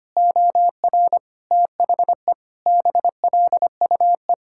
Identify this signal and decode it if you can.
Morse code, heard as a keyed tone, about 25 wpm, 700 Hz. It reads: OR THE BLUE
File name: practice / 25wpm / Sentences with 2-Word Phrases - 25wpm